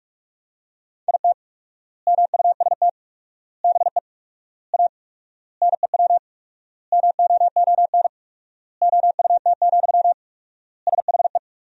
Morse code, heard as a keyed tone, about 42 wpm, 690 Hz. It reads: IT MUST BE A NEW MOON OUT, SHE
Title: 4x Word-Spacing - Sentences from Top 500 Words - 45wpm